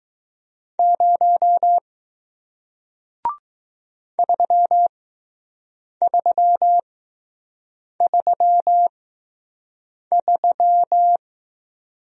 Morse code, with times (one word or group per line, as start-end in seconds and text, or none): 0.8-1.8 0
4.2-11.2 3333